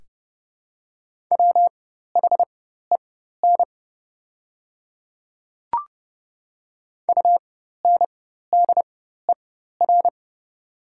Morse code, heard as a keyed tone, about 30 wpm, spaced as for 14 wpm, 700 Hz. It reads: WHEN UNDER
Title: 4x Character Spacing - Top 200 Words - 30wpm